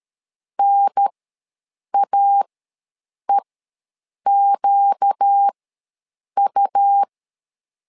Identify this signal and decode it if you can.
Morse code, heard as a keyed tone, about 13 wpm, spaced as for 7 wpm, 800 Hz. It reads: NAEQU
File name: LEZIONI / 4 Q1 ws 12wpm 2129